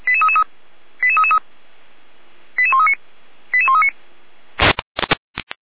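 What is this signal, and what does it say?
Five-tone selective call (ZVEI), heard as quick runs of five tones, tones of about 70 ms. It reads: 88333 88139 88139
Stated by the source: ZVEI five-tone selective call received on an SDR.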